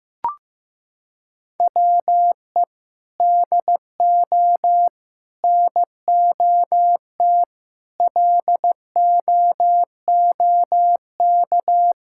Morse code, heard as a keyed tone, about 15 wpm, 700 Hz. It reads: WE DO NOT LOOK
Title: Sentences from Top 1600 Words 15wpm